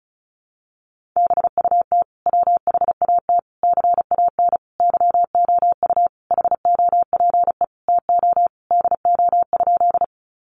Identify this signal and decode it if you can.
Morse code, heard as a keyed tone, about 34 wpm, 700 Hz. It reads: BUT WHAT CAN YOU HOPE TO DO?
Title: Sentences from Top 1000 Words 35wpm